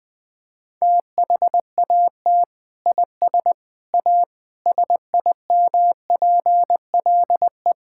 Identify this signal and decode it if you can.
Morse code, heard as a keyed tone, about 20 wpm, 700 Hz. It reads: THAT IS A SIMPLE
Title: Sentences from Top 1000 Words - Review Part 6 - 20wpm